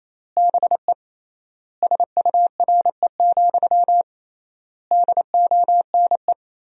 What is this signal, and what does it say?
Morse code, heard as a keyed tone, about 27 wpm, 700 Hz. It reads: BE SURE, DONE
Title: 3x Word-Spacing - Sentences from Top 500 Words - 28wpm